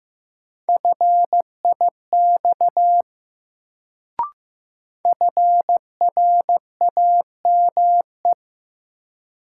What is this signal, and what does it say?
Morse code, heard as a keyed tone, about 15 wpm, 700 Hz. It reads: FIX FRAME
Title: Sentences from Top 1400 Words - New Word Drill Alt - 15wpm